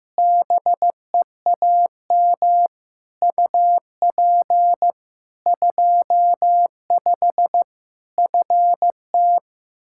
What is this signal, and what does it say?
Morse code, heard as a keyed tone, about 15 wpm, 700 Hz. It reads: BEAM UP 25 FT
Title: Practice QSO - Exchange 2 - Info - 15wpm